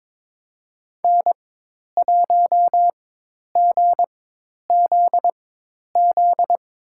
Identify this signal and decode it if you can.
Morse code, heard as a keyed tone, about 23 wpm, spaced as for 10 wpm, 700 Hz. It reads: N1GZZ
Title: USA Call Signs - 4x Character Spacing - 22wpm